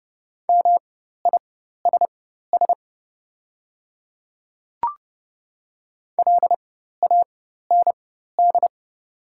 Morse code, heard as a keyed tone, about 30 wpm, spaced as for 14 wpm, 700 Hz. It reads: MISS LAND
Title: 4x Character Spacing - Top 500 Words - 30wpm